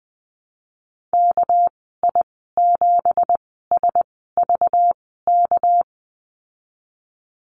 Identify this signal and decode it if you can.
Morse code, heard as a keyed tone, about 20 wpm, spaced as for 15 wpm, 700 Hz. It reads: KI7SVK
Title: 213 Lesson 34g - Number 8 - Call Signs 20wpm